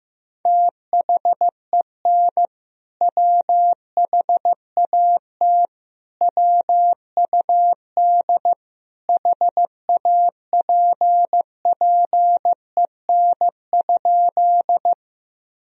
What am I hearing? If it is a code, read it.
Morse code, heard as a keyed tone, about 16 wpm, 700 Hz. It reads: THEN WHAT WUD HAPPEN?
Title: Sentences from Top 400 Words and Abbreviations 15wpm